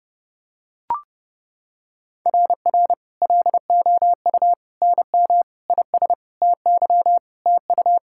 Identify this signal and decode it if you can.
Morse code, heard as a keyed tone, about 30 wpm, 700 Hz. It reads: RR LOU NM IS TY TU
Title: SOTA QSO Practice 30wpm